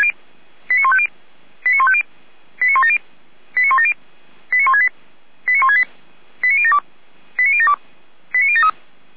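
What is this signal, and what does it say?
Five-tone selective call (ZVEI), heard as quick runs of five tones, tones of about 70 ms. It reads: 89150 89150 89170 89170 89168 89168 89072 89072 89073
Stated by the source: Five-tone ZVEI selective call received on SDR.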